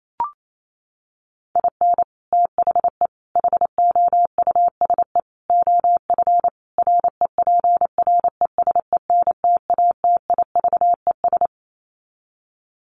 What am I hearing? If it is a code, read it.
Morse code, heard as a keyed tone, about 28 wpm, 700 Hz. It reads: IN THE HOUSE OF REPRESENTATIVES
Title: Sets of 5 Words Encore 28wpm